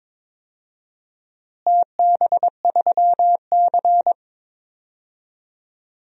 Morse code, heard as a keyed tone, about 22 wpm, 700 Hz. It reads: TB3C
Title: DXCC-100 Call Signs - 4 Characters - Repeated 2x - 22wpm